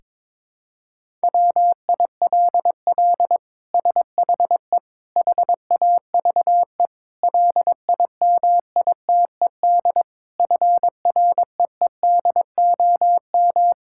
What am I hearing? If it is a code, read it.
Morse code, heard as a keyed tone, about 22 wpm, 700 Hz. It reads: WILL SHE HAVE LIMITED FREEDOM
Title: Sentences from Top 1000 Words - Review Part 6 - 22wpm